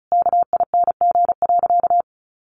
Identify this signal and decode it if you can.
Morse code, heard as a keyed tone, about 34 wpm, 700 Hz. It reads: KING.
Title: Sentences from Top 1400 Words 35wpm